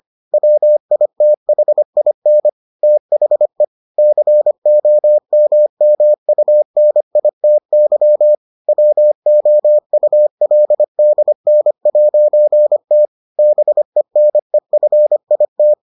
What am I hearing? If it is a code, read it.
Morse code, heard as a keyed tone, about 25 wpm, 600 Hz. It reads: WITHIN THE COMMUNITY WOULDN'T BENEFIT